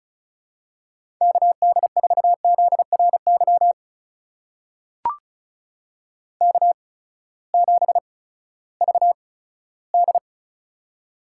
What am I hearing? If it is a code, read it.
Morse code, heard as a keyed tone, about 36 wpm, spaced as for 18 wpm, 700 Hz. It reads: KD4ZRY K7VD
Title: USA Call Signs - 8x Character Spacing - 35wpm